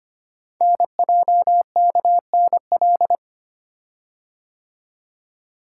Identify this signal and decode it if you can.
Morse code, heard as a keyed tone, about 25 wpm, 700 Hz. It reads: NJKNL
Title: Code Groups - 5 Letter:Numbers - 25wpm